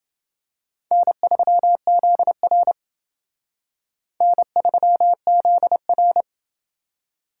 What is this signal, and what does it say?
Morse code, heard as a keyed tone, about 28 wpm, 700 Hz. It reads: N3ZR N3ZR